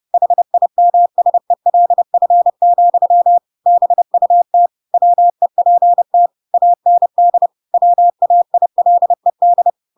Morse code, heard as a keyed tone, about 30 wpm, 700 Hz. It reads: HIMSELF, BUT WEPT AND WAILED
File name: Pinocchio / Pinocchio ch3 30 30 ik1hge